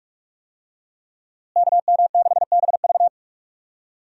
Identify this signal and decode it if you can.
Morse code, heard as a keyed tone, about 46 wpm, 700 Hz. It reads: KM6BV
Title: USA Call Signs - 5 Characters - Repeated 2x - 45wpm